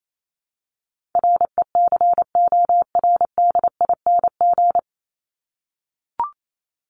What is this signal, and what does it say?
Morse code, heard as a keyed tone, about 28 wpm, 700 Hz. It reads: RECORDING
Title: Top 1501-1750 Words 28wpm